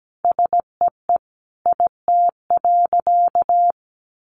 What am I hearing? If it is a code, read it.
Morse code, heard as a keyed tone, about 17 wpm, 700 Hz. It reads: SEE IT.
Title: A Sound Of Thunder 17wpm